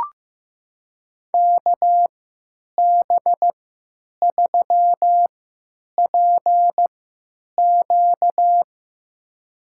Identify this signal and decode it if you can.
Morse code, heard as a keyed tone, about 15 wpm, spaced as for 9 wpm, 700 Hz. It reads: KB3PQ
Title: USA Call Signs - 5 Characters - 3x Character Spacing - 15wpm